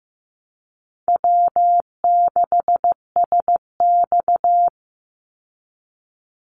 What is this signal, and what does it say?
Morse code, heard as a keyed tone, about 15 wpm, 700 Hz. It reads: W6SX
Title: Easy Call Signs Repeated 2x - 15wpm